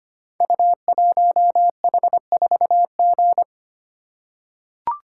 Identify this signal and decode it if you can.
Morse code, heard as a keyed tone, about 25 wpm, 700 Hz. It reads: U1H4G
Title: Code Groups - 5 Letter:Numbers - 25wpm